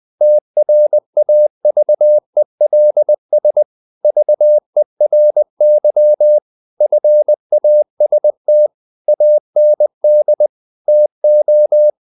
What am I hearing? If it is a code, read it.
Morse code, heard as a keyed tone, about 20 wpm, 600 Hz. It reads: TRAVELS VERY FAST AND TO
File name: the old man and the sea / the old man and the sea part 1 20wpm